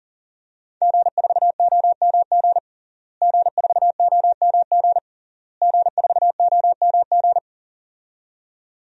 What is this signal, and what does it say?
Morse code, heard as a keyed tone, about 40 wpm, 700 Hz. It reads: G4OMG G4OMG G4OMG
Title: DXCC-100 Call Signs - 5 Characters - Repeated 3x - 40wpm